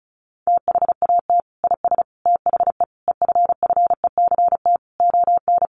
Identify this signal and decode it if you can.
Morse code, heard as a keyed tone, about 34 wpm, 700 Hz. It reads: THAT IS THE EFFECT ON
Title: Sentences from Top 1300 Words 35wpm